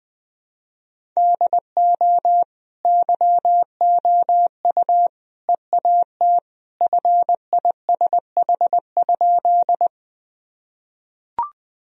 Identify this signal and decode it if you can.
Morse code, heard as a keyed tone, about 20 wpm, 700 Hz. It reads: DO YOU EAT FISH?
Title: Sentences from Top 1000 Words - Review Part 1 - 20wpm